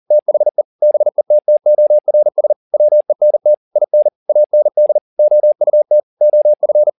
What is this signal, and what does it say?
Morse code, heard as a keyed tone, about 40 wpm, 600 Hz. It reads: THE BETTORS WENT IN AND OUT OF
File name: the old man and the sea 2 40wpm